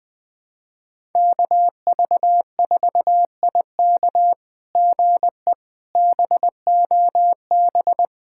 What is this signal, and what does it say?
Morse code, heard as a keyed tone, about 20 wpm, 700 Hz. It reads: KV4IK GE BOB
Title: SOTA QSO Practice 20wpm